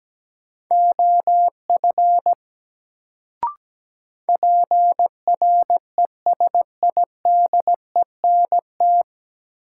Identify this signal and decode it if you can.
Morse code, heard as a keyed tone, about 17 wpm, 700 Hz. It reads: OF PRESIDENT
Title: Sets of 2 Words - New Word Drill - 17wpm